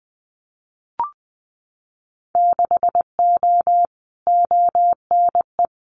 Morse code, heard as a keyed tone, about 20 wpm, 700 Hz. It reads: 6O ONE
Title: Winter Field Day Exchanges Encore 20wpm